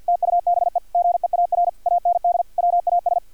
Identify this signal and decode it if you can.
Morse code, heard as a keyed tone, about 50 wpm, 700 Hz, with radio noise. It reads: TUBE GEAR AND WAS
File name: archived / CRSnet